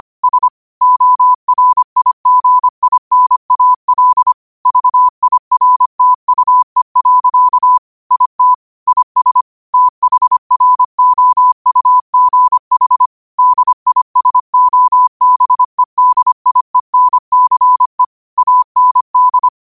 Morse code, heard as a keyed tone, about 25 wpm, 1000 Hz. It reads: S ORIGINAL VIRTUE. IT IS THROUGH DISOBEDIENCE AND
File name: qotdfiles / 25wpm